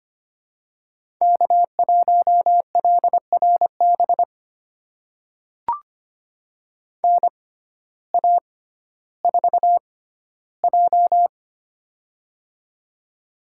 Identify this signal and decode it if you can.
Morse code, heard as a keyed tone, about 25 wpm, spaced as for 14 wpm, 700 Hz. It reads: K1LRB NA4J
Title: USA Call Signs - 6x Character Spacing - 25wpm